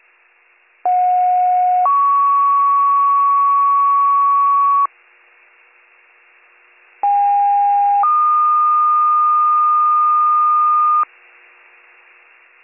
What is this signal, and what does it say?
Two-tone page, heard as a one-second tone, then a three-second tone, about 730 then 1120 Hz, about 810 then 1220 Hz. It is heard from 0.9 to 11.0 s.